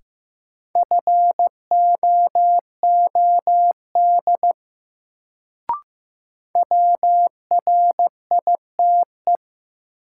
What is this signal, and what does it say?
Morse code, heard as a keyed tone, about 15 wpm, 700 Hz. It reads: FOOD WRITE
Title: Sentences from Top 300 Words - New Word Drill - 15wpm